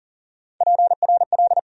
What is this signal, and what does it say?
Morse code, heard as a keyed tone, about 40 wpm, 700 Hz. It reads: PRL
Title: USA Call Sign Postfixes - 3 Letters 40wpm